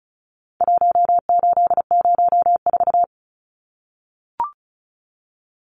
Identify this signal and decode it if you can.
Morse code, heard as a keyed tone, about 34 wpm, 700 Hz. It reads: 1804
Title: Numbers - Contest Style - 4-Digit 35wpm